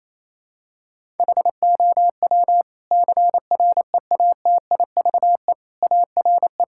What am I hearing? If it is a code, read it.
Morse code, heard as a keyed tone, about 28 wpm, 700 Hz. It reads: HOW CREATIVE ARE
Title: Sentences from Top 1000 Words - Review Part 5 - 28wpm